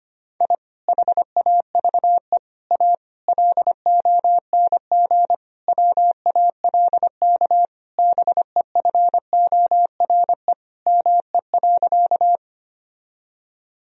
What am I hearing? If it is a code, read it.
Morse code, heard as a keyed tone, about 25 wpm, 700 Hz. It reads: I HAVE A LONG WALK BEFORE ME.
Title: Sentences from Top 1000 Words - Review Part 1 - 25wpm